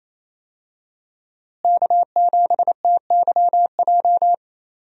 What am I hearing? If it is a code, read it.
Morse code, heard as a keyed tone, about 28 wpm, 700 Hz. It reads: K7TYJ
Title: USA Call Signs - 5 Characters - Repeated 2x - 28wpm